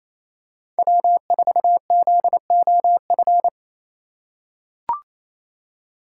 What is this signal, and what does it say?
Morse code, heard as a keyed tone, about 29 wpm, 700 Hz. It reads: W4ZOF
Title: Code Groups - 5 Letter:Numbers - 28wpm